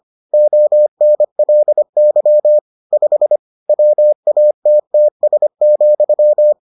Morse code, heard as a keyed tone, about 25 wpm, 600 Hz. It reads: ONLY 5 WATTS,